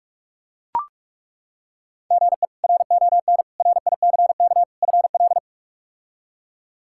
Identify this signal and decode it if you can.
Morse code, heard as a keyed tone, about 43 wpm, 700 Hz. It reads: GE RON RICK FL
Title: K1USN SST QSO Practice - 45wpm